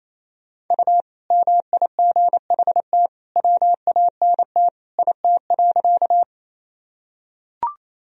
Morse code, heard as a keyed tone, about 28 wpm, 700 Hz. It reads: U MIGHT WANT IT.
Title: Sentences from Top 300 Words and Abbreviations 28wpm